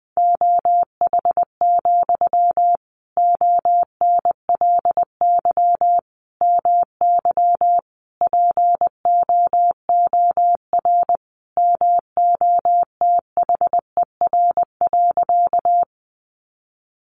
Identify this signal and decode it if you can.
Morse code, heard as a keyed tone, about 20 wpm, 700 Hz. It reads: OH, ONLY MY POOR MOTHER.
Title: Sentences from Top 1000 Words 20wpm